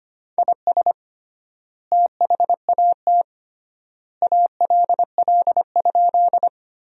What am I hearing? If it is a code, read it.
Morse code, heard as a keyed tone, about 25 wpm, 700 Hz. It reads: IS THAT ALL?
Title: 3x Word-Spacing - Sentences from Top 100 Words - 25wpm